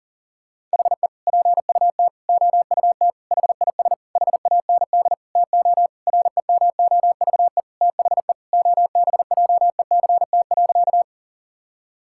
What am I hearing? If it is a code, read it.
Morse code, heard as a keyed tone, about 37 wpm, 700 Hz. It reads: HE PUT OUT HIS HAND TO REMOVE THE OBJECT.
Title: Sentences from Top 1000 Words - Review Part 5 - 40wpm